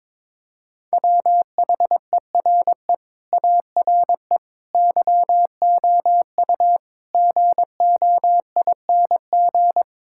Morse code, heard as a keyed tone, about 21 wpm, 700 Hz. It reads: WHERE ARE YOU GOING
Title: Sentences from Top 1600 Words 22wpm